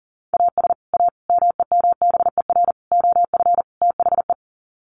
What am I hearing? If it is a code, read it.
Morse code, heard as a keyed tone, about 40 wpm, 700 Hz. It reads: AS A MEMBER OF THE
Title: Sets of 5 Words 40wpm